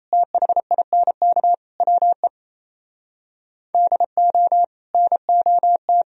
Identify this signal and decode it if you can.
Morse code, heard as a keyed tone, about 34 wpm, 700 Hz. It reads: THINK WE DO NOT
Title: Speed-Racing - Sets of 4 Words 28wpm